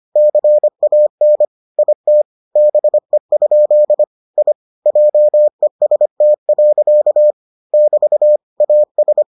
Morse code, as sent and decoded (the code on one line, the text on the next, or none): -.-. .- -. / .. - / -... ..--.. / .. / .--- . ... - .-.-.- / -...- / .- ...
CAN IT BE? I JEST. <BT> AS